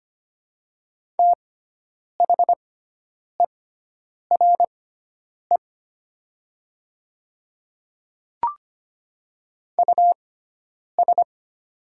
Morse code, heard as a keyed tone, about 25 wpm, spaced as for 9 wpm, 700 Hz. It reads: THERE US
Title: 6x Character Spacing - Top 200 Words - 25wpm